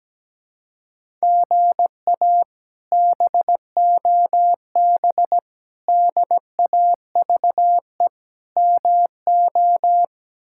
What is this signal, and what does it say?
Morse code, heard as a keyed tone, about 17 wpm, 700 Hz. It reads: GA BOB DAVE MO
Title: K1USN SST QSO Practice - 17wpm